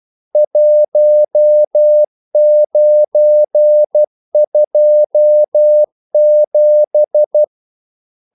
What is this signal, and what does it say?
Morse code, heard as a keyed tone, about 12 wpm, 600 Hz. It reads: 1927
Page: English quotations in CW (telegraphy - morse)